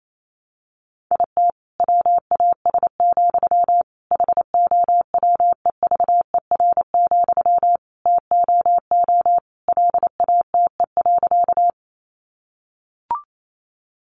Morse code, heard as a keyed tone, about 28 wpm, 700 Hz. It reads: IT WAS, HOWEVER, TOO LATE.
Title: Sentences from Top 600 Words 28wpm